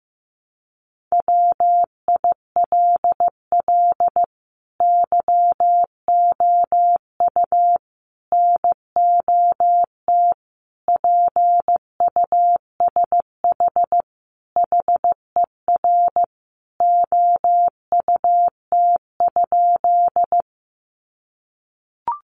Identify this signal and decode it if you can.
Morse code, heard as a keyed tone, about 15 wpm, 700 Hz. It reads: WILL YOU NOT PUSH HER OUT?
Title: Sentences from Top 1300 Words 15wpm